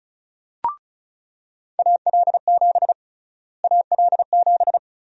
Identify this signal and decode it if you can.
Morse code, heard as a keyed tone, about 34 wpm, 700 Hz. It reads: AL7 AL7
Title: USA Call Sign Prefixes - Repeated 2x - 35wpm